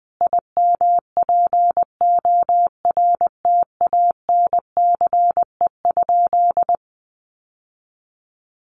Morse code, heard as a keyed tone, about 20 wpm, 700 Hz. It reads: IMPORTANCE?
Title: Sentences from Top 1200 Words 20wpm